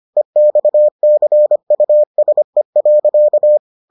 Morse code, heard as a keyed tone, about 25 wpm, 600 Hz. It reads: EXCUSE.